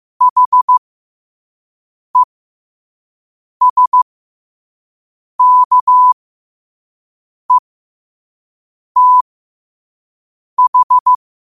Morse code, heard as a keyed tone, about 15 wpm, spaced as for 5 wpm, 1000 Hz. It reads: HESKETH
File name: qotdfiles / 5wpm